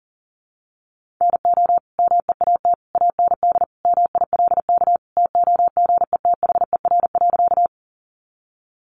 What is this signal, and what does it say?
Morse code, heard as a keyed tone, about 40 wpm, 700 Hz. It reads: NO MEAT AND MILK TOGETHER.
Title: Sentences from Top 1500 Words 40wpm